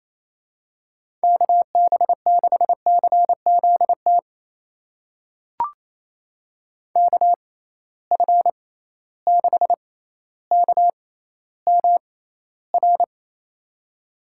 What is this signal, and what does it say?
Morse code, heard as a keyed tone, about 28 wpm, spaced as for 14 wpm, 700 Hz. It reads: KB6CZT KF6KMR
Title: USA Call Signs - 6 Characters - 6x Character Spacing - 28wpm